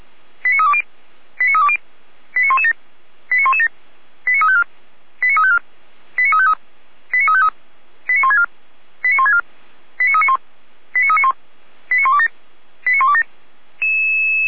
Five-tone selective call (ZVEI), heard as quick runs of five tones, tones of about 70 ms. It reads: 89320 89320 89117 89117 89354 89354 89353 89353 89164 89164 89391 89391 89127 89127